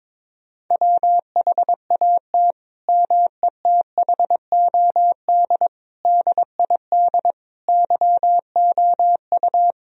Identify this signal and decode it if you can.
Morse code, heard as a keyed tone, about 22 wpm, 700 Hz. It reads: WHAT METHOD DID YOU
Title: Sentences from Top 1000 Words - Review Part 3 - 22wpm